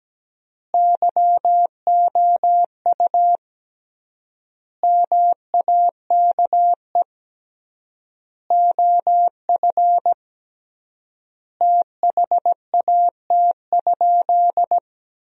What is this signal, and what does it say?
Morse code, heard as a keyed tone, about 17 wpm, 700 Hz. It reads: YOU MAKE OF THAT?
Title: 3x Word-Spacing - Sentences from Top 100 Words - 17wpm